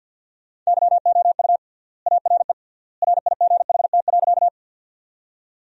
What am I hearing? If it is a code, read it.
Morse code, heard as a keyed tone, about 53 wpm, 700 Hz. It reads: YOU ARE RIGHT.
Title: 3x Word-Spacing - Sentences from Top 200 Words - 50wpm